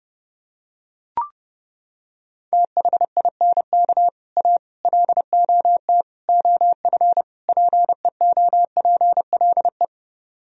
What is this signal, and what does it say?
Morse code, heard as a keyed tone, about 28 wpm, 700 Hz. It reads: THINK A LOT OF PEOPLE